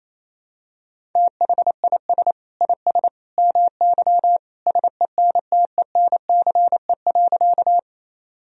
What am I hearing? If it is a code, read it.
Morse code, heard as a keyed tone, about 27 wpm, 700 Hz. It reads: THIS IS MY SENTENCE.
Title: Sentences from Top 1000 Words - Review Part 1 - 28wpm